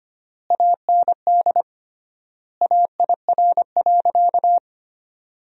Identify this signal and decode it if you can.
Morse code, heard as a keyed tone, about 25 wpm, 700 Hz. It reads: AND AIR.
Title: 3x Word-Spacing - Sentences from Top 400 Words - 25wpm